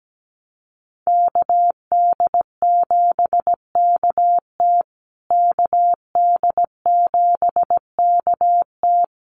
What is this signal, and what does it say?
Morse code, heard as a keyed tone, about 17 wpm, 700 Hz. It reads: KD7KT KD7KT
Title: USA Call Signs Repeated 2x 17wpm